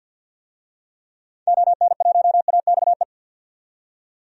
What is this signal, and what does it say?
Morse code, heard as a keyed tone, about 50 wpm, 700 Hz. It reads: ON1AXE
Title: DXCC-100 Call Signs - 6 Characters - 50wpm